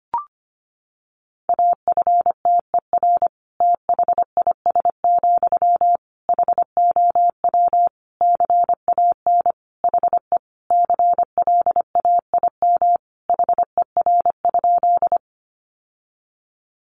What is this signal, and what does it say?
Morse code, heard as a keyed tone, about 25 wpm, 700 Hz. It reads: AFTER THIS, HOW CAN HE CLAIM HER?
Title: Sentences from Top 1500 Words 25wpm